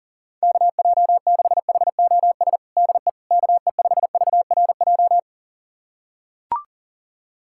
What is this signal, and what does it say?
Morse code, heard as a keyed tone, about 40 wpm, 700 Hz. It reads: KJ6HOS DE KE5VRJ
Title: USA Callsign DE Callsign - 6 Characters - 40wpm